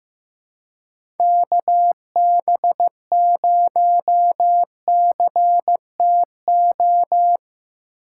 Morse code, heard as a keyed tone, about 15 wpm, 700 Hz. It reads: KB0CTO
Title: USA Call Signs - 6 Characters - 6x Character Spacing - 15wpm